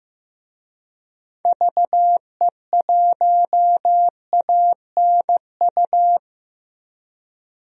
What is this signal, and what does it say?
Morse code, heard as a keyed tone, about 14 wpm, 700 Hz. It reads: VE1ANU
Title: DXCC-100 Call Signs - 6 Characters - 15wpm